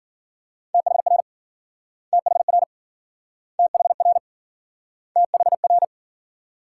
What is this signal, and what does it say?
Morse code, heard as a keyed tone, about 53 wpm, 700 Hz. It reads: THR THR THR THR